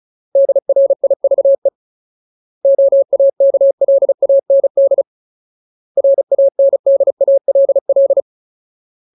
Morse code, heard as a keyed tone, about 34 wpm, 550 Hz. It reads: DRIVE OAKLAND RANDALL
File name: Ultimate Word List / 35WPM Ultimate